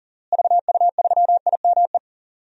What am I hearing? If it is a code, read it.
Morse code, heard as a keyed tone, about 39 wpm, 700 Hz. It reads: VU3IME
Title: DXCC-100 Call Signs - 6 Characters - Repeated 3x - 40wpm